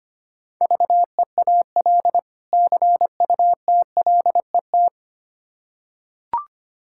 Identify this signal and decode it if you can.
Morse code, heard as a keyed tone, about 25 wpm, 700 Hz. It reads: VEAL CUTLET
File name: practice / 25wpm / Difficult Words with Hints - 25wpm